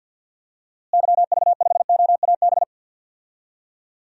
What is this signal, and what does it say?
Morse code, heard as a keyed tone, about 51 wpm, 700 Hz. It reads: YV5OAB
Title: DXCC-100 Call Signs - 6 Characters - Repeated 3x - 50wpm